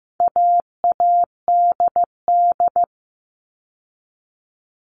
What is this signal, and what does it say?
Morse code, heard as a keyed tone, about 15 wpm, 700 Hz. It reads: AADD
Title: All Cut Numbers - 4-Digit 15wpm